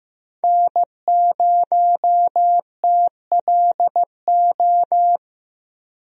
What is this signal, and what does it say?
Morse code, heard as a keyed tone, about 15 wpm, 700 Hz. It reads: N0TLO
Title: USA Call Signs - 5 Characters - 15wpm